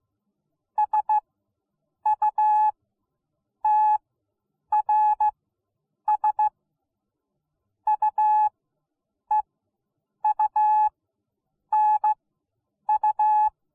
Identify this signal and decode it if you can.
Morse code, heard as a keyed tone, about 16 wpm, spaced as for 8 wpm, 850 Hz. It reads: SUTRS UEUNU